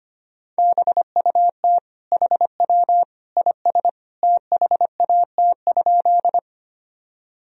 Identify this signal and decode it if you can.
Morse code, heard as a keyed tone, about 25 wpm, 700 Hz. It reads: BUT HW IS THAT?
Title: Sentences from Top 100 Words and Abbreviations 25wpm